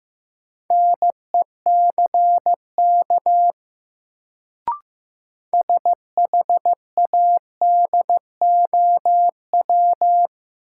Morse code, heard as a keyed tone, about 15 wpm, 700 Hz. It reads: NECK SHADOW
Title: Sets of 5 Words Encore - New Word Drill - 15wpm